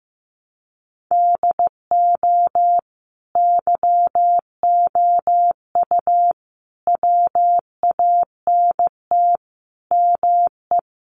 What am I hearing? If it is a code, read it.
Morse code, heard as a keyed tone, about 14 wpm, 700 Hz. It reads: DO YOU WANT ME
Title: Sentences from Top 600 Words 15wpm